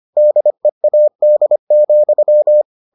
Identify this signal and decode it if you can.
Morse code, heard as a keyed tone, about 25 wpm, 600 Hz. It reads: DEAD,